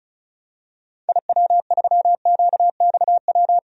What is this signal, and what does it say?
Morse code, heard as a keyed tone, about 34 wpm, 700 Hz. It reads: IW3QXW
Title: DXCC-100 Call Signs - 6 Characters - 35wpm